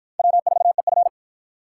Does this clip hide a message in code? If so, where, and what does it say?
Yes: Morse code from 0.2 to 1.1 s, heard as a keyed tone, about 49 wpm, 700 Hz. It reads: W4EF